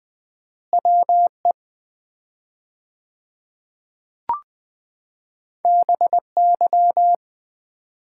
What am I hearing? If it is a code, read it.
Morse code, heard as a keyed tone, about 20 wpm, 700 Hz. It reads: WE BY
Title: 4x Word-Spacing - Sets of 2 Words - 20wpm